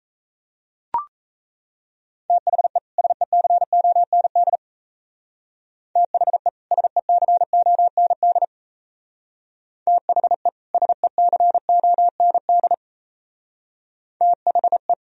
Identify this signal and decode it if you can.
Morse code, heard as a keyed tone, about 34 wpm, 690 Hz. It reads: THE SECOND THE SECOND THE SECOND THE
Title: Speed-Racing - Sets of 2 Words 28wpm